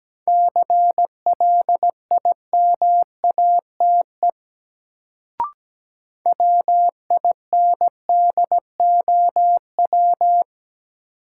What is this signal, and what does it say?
Morse code, heard as a keyed tone, about 17 wpm, 700 Hz. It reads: CLIMATE WINDOW
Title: Sentences from Top 1600 Words - New Word Drill Alt - 17wpm